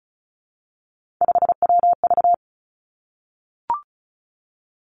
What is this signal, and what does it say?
Morse code, heard as a keyed tone, about 36 wpm, 700 Hz. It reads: 5WV
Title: Three Challenging Letter-Number Combinations 35wpm